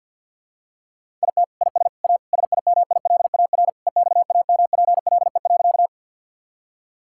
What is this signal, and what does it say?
Morse code, heard as a keyed tone, about 50 wpm, 700 Hz. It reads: IT IS A SIMILAR EXAMPLE.
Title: Sentences from Top 1000 Words - Review Part 5 - 50wpm